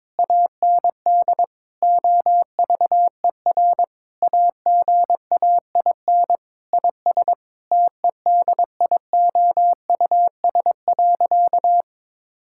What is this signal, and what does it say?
Morse code, heard as a keyed tone, about 22 wpm, 700 Hz. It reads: AND OVER AGAIN IS TEDIOUS.
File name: Sentences with 7-Word Phrases - 22wpm